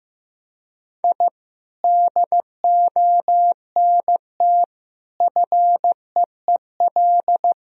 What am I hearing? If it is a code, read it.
Morse code, heard as a keyed tone, about 15 wpm, 700 Hz. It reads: I DONT FEEL
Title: Sentences from Top 1000 Words - Review Part 3 - 15wpm